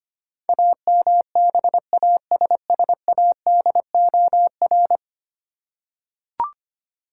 Morse code, heard as a keyed tone, about 25 wpm, 700 Hz. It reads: AMBASSADOR
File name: Ten Letter Words - 25wpm